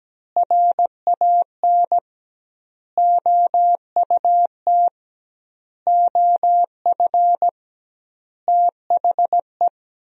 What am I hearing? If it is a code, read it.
Morse code, heard as a keyed tone, about 17 wpm, 700 Hz. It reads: RAN OUT OF THE
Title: 2x Word-Spacing - Sentences from Top 500 Words - 17wpm